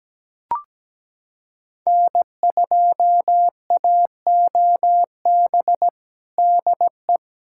Morse code, heard as a keyed tone, about 17 wpm, 700 Hz. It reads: N2AOB DE
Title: USA Callsign DE Callsign - 5 Characters - 17wpm